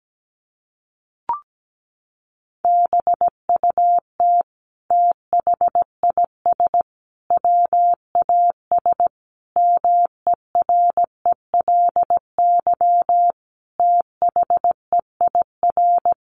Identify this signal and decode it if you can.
Morse code, heard as a keyed tone, about 17 wpm, 700 Hz. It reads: BUT THIS WAS MERELY THEIR
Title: Sentences from Top 1400 Words 17wpm